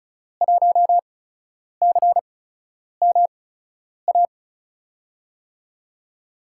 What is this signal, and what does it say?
Morse code, heard as a keyed tone, about 36 wpm, spaced as for 10 wpm, 700 Hz. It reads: 1CMA
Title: USA Call Signs - 5 Characters - 8x Character Spacing - 35wpm